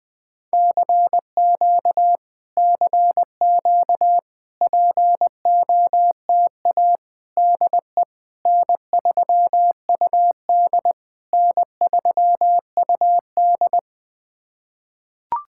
Morse code, heard as a keyed tone, about 20 wpm, 690 Hz. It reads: CQ CQ POTA DE N3UD N3UD
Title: POTA QSO Practice - 20wpm